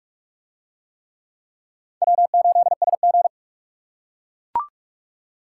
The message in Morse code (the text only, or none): W8SG